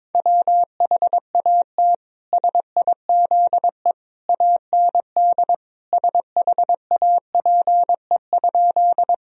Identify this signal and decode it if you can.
Morse code, heard as a keyed tone, about 22 wpm, 700 Hz. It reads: WHAT SIZE AND SHAPE?
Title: Sentences from Top 1000 Words - Review Part 3 - 22wpm